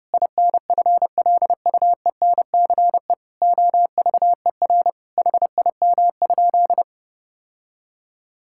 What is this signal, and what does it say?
Morse code, heard as a keyed tone, about 30 wpm, 700 Hz. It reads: INFLUENCE OVER HIM?
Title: Sentences from Top 1600 Words 30wpm